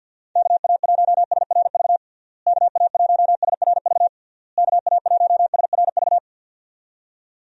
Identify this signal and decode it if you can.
Morse code, heard as a keyed tone, about 50 wpm, 700 Hz. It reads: KA1SRV KA1SRV KA1SRV
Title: USA Call Signs - 6 Characters - Repeated 3x - 50wpm